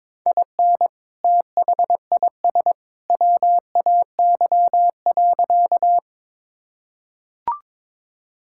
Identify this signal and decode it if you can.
Morse code, heard as a keyed tone, about 22 wpm, 700 Hz. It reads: IN THIS WAY.
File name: practice / 22wpm / Sentences with 2-Word Phrases - 22wpm